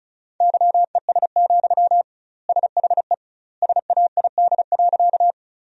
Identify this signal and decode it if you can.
Morse code, heard as a keyed tone, about 34 wpm, 700 Hz. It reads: YES, SHE SAID.
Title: 2x Word-Spacing - Sentences from Top 500 Words - 35wpm